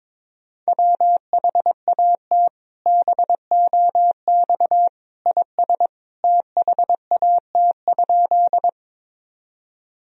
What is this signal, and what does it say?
Morse code, heard as a keyed tone, about 22 wpm, 700 Hz. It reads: WHAT BOX IS THAT?
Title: Sentences from Top 1000 Words - Review Part 2 - 22wpm